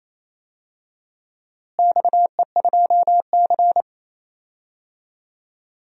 Morse code, heard as a keyed tone, about 28 wpm, 700 Hz. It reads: XE2C
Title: DXCC-100 Call Signs - 4 Characters - Repeated 3x - 28wpm